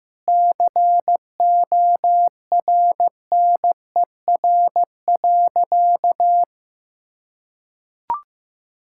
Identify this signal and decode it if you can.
Morse code, heard as a keyed tone, about 15 wpm, 700 Hz. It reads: CORNER.
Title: Sentences from Top 1600 Words 15wpm